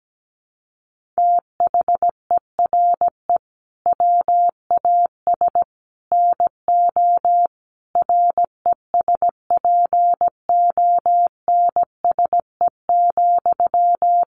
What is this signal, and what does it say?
Morse code, heard as a keyed tone, about 17 wpm, 700 Hz. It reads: THERE WAS NO RESPONSE,
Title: Sentences from Top 1400 Words 17wpm